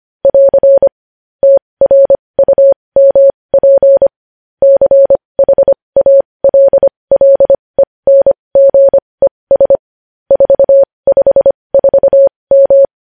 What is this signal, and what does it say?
Morse code, heard as a keyed tone, about 25 wpm, spaced as for 20 wpm, 550 Hz. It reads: <AR> TRUMP CHALLENGES 454M
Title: News Headlines at 20